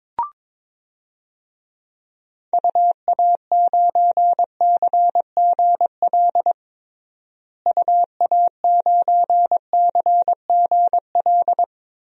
Morse code, heard as a keyed tone, about 22 wpm, 700 Hz. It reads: UA9CGL UA9CGL
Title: DXCC-100 Call Signs - 6 Characters - Repeated 3x - 22wpm